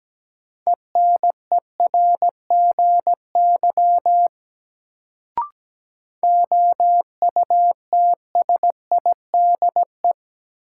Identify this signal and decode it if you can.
Morse code, heard as a keyed tone, about 17 wpm, 700 Hz. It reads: ENERGY OUTSIDE
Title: Sentences from Top 900 Words - New Word Drill - 17wpm